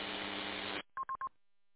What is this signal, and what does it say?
There is no code.